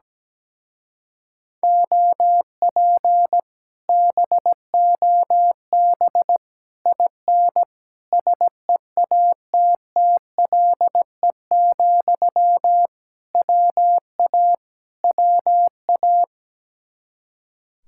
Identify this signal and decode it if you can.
Morse code, heard as a keyed tone, about 17 wpm, 700 Hz. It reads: OP BOB IN SEATTLE, WA WA
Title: Speed-Racing Common QSO Phrases Encore - 17wpm